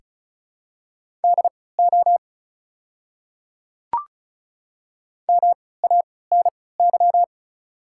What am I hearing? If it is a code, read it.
Morse code, heard as a keyed tone, about 36 wpm, spaced as for 20 wpm, 700 Hz. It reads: DO MANY
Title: 3x Character Spacing - Top 300 Words - 35wpm